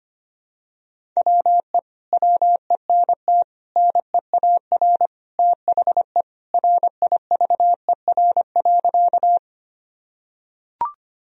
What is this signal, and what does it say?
Morse code, heard as a keyed tone, about 25 wpm, 700 Hz. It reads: WE WENT NEAR THE RIVER.
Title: Sentences from Top 1000 Words - Review Part 1 - 25wpm